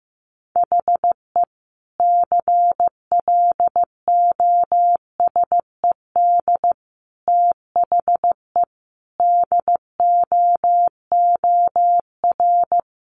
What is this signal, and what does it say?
Morse code, heard as a keyed tone, about 15 wpm, 700 Hz. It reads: HE CLOSED THE DOOR
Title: Sentences from Top 1400 Words 15wpm